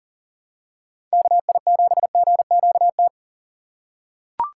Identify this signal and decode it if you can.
Morse code, heard as a keyed tone, about 40 wpm, 700 Hz. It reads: KI7GQT
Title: USA Call Signs - 6 Characters - 2x Character Spacing - 40wpm